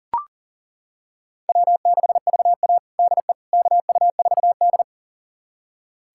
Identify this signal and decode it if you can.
Morse code, heard as a keyed tone, about 40 wpm, 700 Hz. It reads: W6VA DE KU4D
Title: USA Callsign DE Callsign - 4 Characters - 40wpm